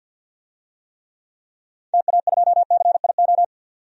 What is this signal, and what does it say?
Morse code, heard as a keyed tone, about 46 wpm, 700 Hz. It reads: TA2CIO
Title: DXCC-100 Call Signs - 6 Characters - 50wpm